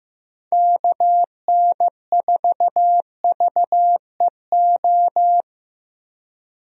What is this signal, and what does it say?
Morse code, heard as a keyed tone, about 16 wpm, 700 Hz. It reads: KN4VEO
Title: USA Call Signs - 6 Characters - Repeated 3x - 15wpm